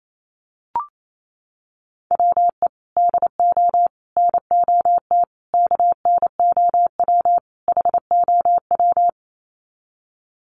Morse code, heard as a keyed tone, about 28 wpm, 690 Hz. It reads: WE DO NOT KNOW HOW
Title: Sets of 5 Words Encore 28wpm